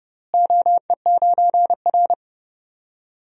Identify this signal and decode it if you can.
Morse code, heard as a keyed tone, about 29 wpm, 700 Hz. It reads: OE9R
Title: DXCC-100 Call Signs - 4 Characters - Repeated 3x - 30wpm